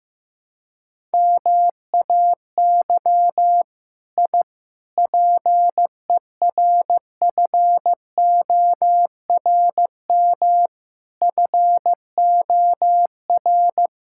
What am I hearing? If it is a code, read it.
Morse code, heard as a keyed tone, about 15 wpm, 700 Hz. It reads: MAY I PERFORM FOR
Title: Sentences from Top 1000 Words - Review Part 5 - 15wpm